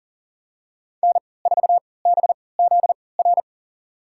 Morse code, heard as a keyed tone, about 40 wpm, spaced as for 23 wpm, 700 Hz. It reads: N4BZR
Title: USA Call Signs - 3x Character Spacing - 40wpm